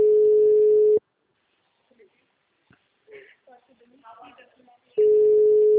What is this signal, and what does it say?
Ringback tone (Europe, 425 Hz), with speech.